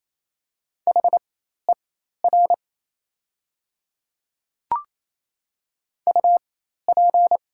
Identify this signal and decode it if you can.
Morse code, heard as a keyed tone, about 27 wpm, spaced as for 13 wpm, 700 Hz. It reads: HER UP
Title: 4x Character Spacing - Top 100 Words - 28wpm